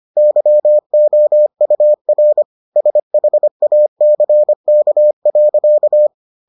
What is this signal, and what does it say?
Morse code, heard as a keyed tone, about 25 wpm, 600 Hz. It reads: YOUR SHACK.